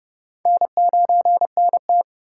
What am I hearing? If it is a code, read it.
Morse code, heard as a keyed tone, about 29 wpm, 700 Hz. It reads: N9NT